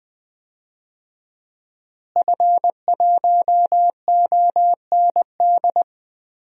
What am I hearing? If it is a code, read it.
Morse code, heard as a keyed tone, about 21 wpm, 700 Hz. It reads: F1OND
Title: DXCC-100 Call Signs - 5 Characters - Repeated 2x - 20wpm